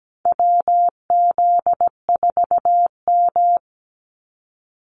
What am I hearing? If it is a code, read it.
Morse code, heard as a keyed tone, about 17 wpm, 700 Hz. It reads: WZ4M
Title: NA SOTA Activator Callsigns 17wpm